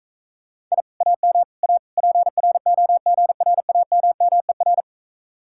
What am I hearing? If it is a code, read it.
Morse code, heard as a keyed tone, about 45 wpm, 700 Hz. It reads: I AM A PROGRAMMER